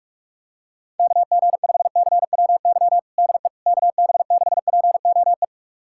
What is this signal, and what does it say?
Morse code, heard as a keyed tone, about 46 wpm, 700 Hz. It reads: KG5CWY DE KB6POE
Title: USA Callsign DE Callsign - 6 Characters - 45wpm